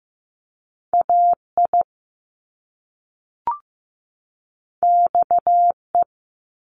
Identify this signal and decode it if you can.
Morse code, heard as a keyed tone, about 15 wpm, 700 Hz. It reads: AI XE
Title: Top Two Letter Combinations in Words 15wpm